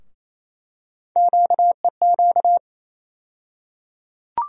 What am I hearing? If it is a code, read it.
Morse code, heard as a keyed tone, about 28 wpm, 700 Hz. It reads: QEQ
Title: USA Call Sign Postfixes - 3 Letters 28wpm